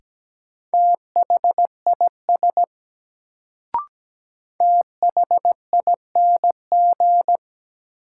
Morse code, heard as a keyed tone, about 17 wpm, 700 Hz. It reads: THIS THING
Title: Sentences from Top 100 Words - New Word Drill - 17wpm